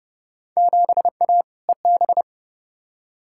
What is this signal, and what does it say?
Morse code, heard as a keyed tone, about 31 wpm, 700 Hz. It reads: 7A EB
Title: Field Day Exchanges Encore - 30wpm